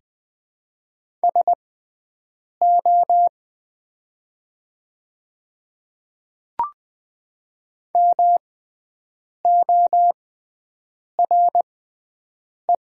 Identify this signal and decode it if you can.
Morse code, heard as a keyed tone, about 20 wpm, spaced as for 7 wpm, 700 Hz. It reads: SO MORE
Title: 6x Character Spacing - Top 100 Words - 20wpm